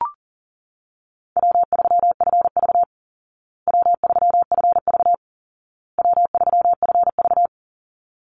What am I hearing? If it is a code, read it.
Morse code, heard as a keyed tone, about 40 wpm, 700 Hz. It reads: W3FV W3FV W3FV
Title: Easy Call Signs Repeated 3x - 40wpm